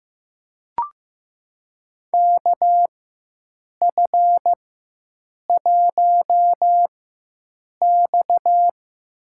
Morse code, heard as a keyed tone, about 15 wpm, spaced as for 7 wpm, 700 Hz. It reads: KF1X
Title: USA Call Signs - 4 Characters - 4x Character Spacing - 15wpm